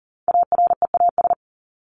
Morse code, heard as a keyed tone, about 40 wpm, 700 Hz. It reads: AREAS
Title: ES:S Suffix - Top 100 Words with Base Words 40wpm